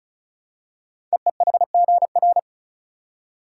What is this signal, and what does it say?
Morse code, heard as a keyed tone, about 36 wpm, 690 Hz. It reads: EEHGR
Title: Code Groups - 5 Letters - 35wpm